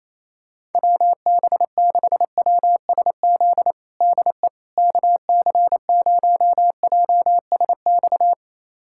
Morse code, heard as a keyed tone, about 28 wpm, 700 Hz. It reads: WB6WSZ DE KC0JSX
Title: USA Callsign DE Callsign 28wpm